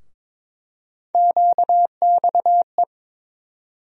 Morse code, heard as a keyed tone, about 22 wpm, 700 Hz. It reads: QXE
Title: USA Call Sign Postfixes - 3 Letters 22wpm